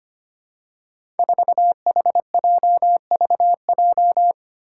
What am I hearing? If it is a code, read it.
Morse code, heard as a keyed tone, about 25 wpm, 700 Hz. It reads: 4HJVJ
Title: Five Challenging Letter-Number Combinations 25wpm